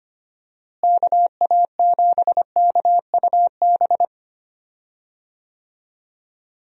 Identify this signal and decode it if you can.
Morse code, heard as a keyed tone, about 25 wpm, 700 Hz. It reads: KA7KUB
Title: USA Call Signs - 6 Characters - Repeated 2x - 25wpm